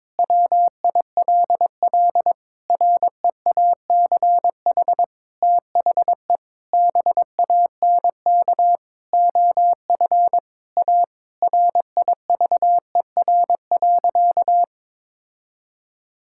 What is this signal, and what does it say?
Morse code, heard as a keyed tone, about 22 wpm, 700 Hz. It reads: WILL REACH THE BANK OF A RIVER.
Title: Sentences from Top 1000 Words - Review Part 6 - 22wpm